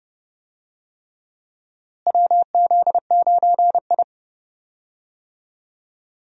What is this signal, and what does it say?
Morse code, heard as a keyed tone, about 31 wpm, 700 Hz. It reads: WZ9I
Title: USA Call Signs - 4 Characters - Repeated 3x - 30wpm